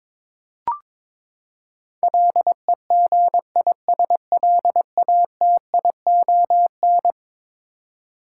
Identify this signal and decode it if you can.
Morse code, heard as a keyed tone, about 22 wpm, 700 Hz. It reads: LEGISLATION
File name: practice / 22wpm / Eleven Letter Words - 22wpm